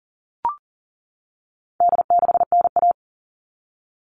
Morse code, heard as a keyed tone, about 40 wpm, 700 Hz. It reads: D6NA
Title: All Cut Numbers - 4-Digit 40wpm